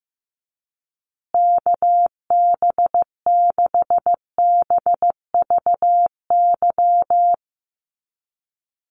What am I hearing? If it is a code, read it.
Morse code, heard as a keyed tone, about 15 wpm, 700 Hz. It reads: KB6BVY
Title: USA Call Signs Repeated 2x 15wpm